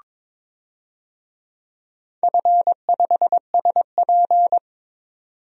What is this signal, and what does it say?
Morse code, heard as a keyed tone, about 22 wpm, 700 Hz. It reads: F5SP
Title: DXCC-100 Call Signs - 4 Characters - 22wpm